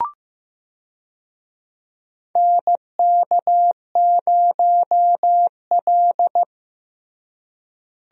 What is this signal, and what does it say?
Morse code, heard as a keyed tone, about 15 wpm, 700 Hz. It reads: NK0L
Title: USA Call Signs - 4 Characters - Repeated 3x - 15wpm